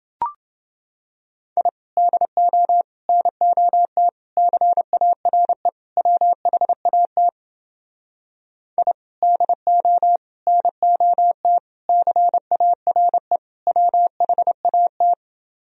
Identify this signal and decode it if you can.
Morse code, heard as a keyed tone, about 30 wpm, 700 Hz. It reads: I DO NOT CARE WHAT I DO NOT CARE WHAT